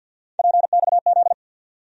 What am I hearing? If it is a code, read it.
Morse code, heard as a keyed tone, about 53 wpm, 700 Hz. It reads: PXZ
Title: USA Call Sign Postfixes - 3 Letters 50wpm